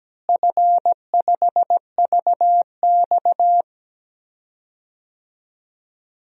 Morse code, heard as a keyed tone, about 17 wpm, 700 Hz. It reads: F5VX
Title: DXCC-100 Call Signs - 4 Characters - Repeated 2x - 17wpm